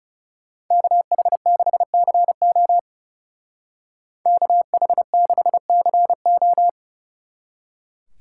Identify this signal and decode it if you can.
Morse code, heard as a keyed tone, about 36 wpm, 700 Hz. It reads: KH6CO KH6CO
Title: Speed Racing - USA 2x2 Call Signs 30wpm